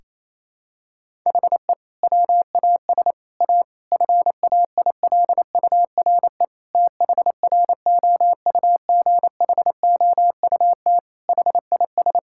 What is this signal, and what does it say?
Morse code, heard as a keyed tone, about 28 wpm, 700 Hz. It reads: HE WAS A FAILURE THROUGHOUT HIS
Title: Sentences from Top 1000 Words - Review Part 6 - 28wpm